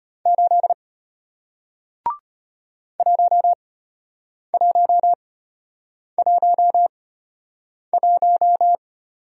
Morse code, heard as a keyed tone, about 33 wpm, spaced as for 8 wpm, 700 Hz. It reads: 8 1111